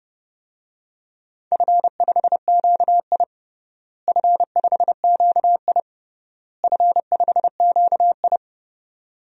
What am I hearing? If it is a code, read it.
Morse code, heard as a keyed tone, about 32 wpm, 700 Hz. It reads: F5QI F5QI F5QI
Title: DXCC-100 Call Signs - 4 Characters - Repeated 3x - 30wpm